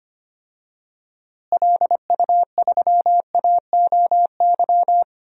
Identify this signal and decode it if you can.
Morse code, heard as a keyed tone, about 25 wpm, 700 Hz. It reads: LU3AOY
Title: DXCC-100 Call Signs - 6 Characters - Repeated 3x - 25wpm